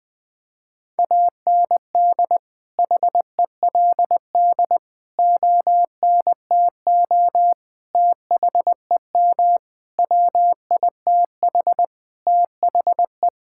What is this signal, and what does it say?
Morse code, heard as a keyed tone, about 21 wpm, 700 Hz. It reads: AND HELD ONTO THEM WITH THE
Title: Sentences with 9-Word Phrases - 20wpm